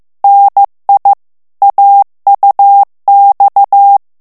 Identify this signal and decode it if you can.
Morse code, heard as a keyed tone, about 15 wpm, 800 Hz. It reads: NI AUX
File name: morse mp3 / FRANCAIS TEXTE CLAIR 2 VITESSE 14 MOTS MINUTES